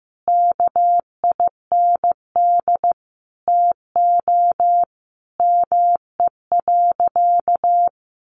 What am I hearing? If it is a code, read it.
Morse code, heard as a keyed tone, about 15 wpm, 700 Hz. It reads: KIND TO ME.
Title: Sentences from Top 1100 Words 15wpm